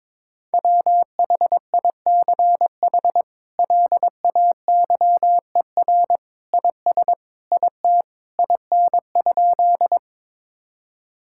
Morse code, heard as a keyed tone, about 21 wpm, 700 Hz. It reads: WHICH LAYER IS IT IN?
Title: Sentences from Top 1600 Words 22wpm